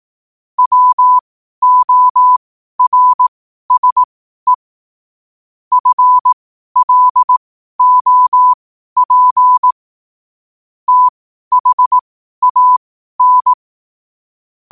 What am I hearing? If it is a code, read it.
Morse code, heard as a keyed tone, about 18 wpm, spaced as for 12 wpm, 1000 Hz. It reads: WORSE FLOP THAN
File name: qotdfiles / 12wpm